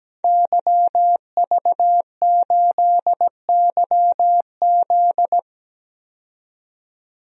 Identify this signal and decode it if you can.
Morse code, heard as a keyed tone, about 17 wpm, 700 Hz. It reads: YV8YZ
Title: DXCC-100 Call Signs - 5 Characters - 17wpm